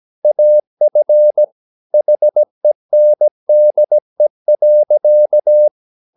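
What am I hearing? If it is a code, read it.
Morse code, heard as a keyed tone, about 17 wpm, 600 Hz. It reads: AF HENDE.
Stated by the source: Slow (QRS) Morse practice at 17 wpm, from CW med Gnister.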